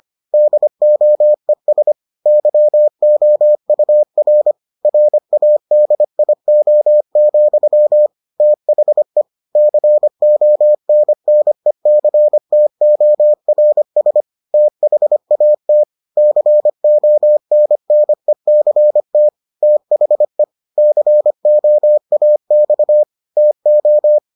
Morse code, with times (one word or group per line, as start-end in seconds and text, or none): 0.3-1.9 DOES
2.3-4.5 YOUR
4.9-8.1 RADIO,
8.4-9.2 THE
9.6-14.2 CONNECTORS
14.5-15.8 THAT
16.2-19.3 CONNECT
19.6-20.4 THE
20.8-23.0 COAX
23.4-24.2 TO